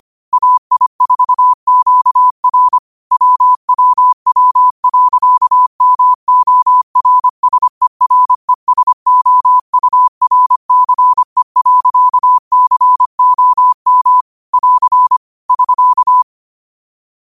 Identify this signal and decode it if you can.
Morse code, heard as a keyed tone, about 25 wpm, 1000 Hz. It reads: AI4QR WWW.MORSERESOURCE.COM <AR> <SK>
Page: qotd-at-25-wpm